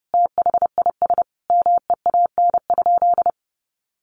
Morse code, heard as a keyed tone, about 30 wpm, 700 Hz. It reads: THIS MEAN?
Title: Sentences from Top 200 Words 30wpm